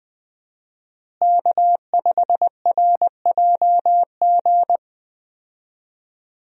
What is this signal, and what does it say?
Morse code, heard as a keyed tone, about 20 wpm, 700 Hz. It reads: K5RJG
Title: USA Call Signs - 5 Characters - Repeated 3x - 20wpm